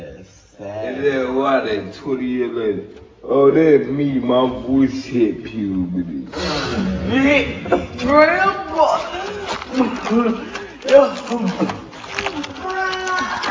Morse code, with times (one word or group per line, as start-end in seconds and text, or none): none